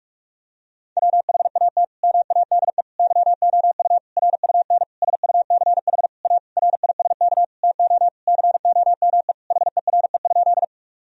Morse code, heard as a keyed tone, about 41 wpm, 700 Hz. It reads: WHAT MADE YOU RUN SUCH A RISK TO COME HERE?
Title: Sentences from Top 1000 Words - Review Part 4 - 45wpm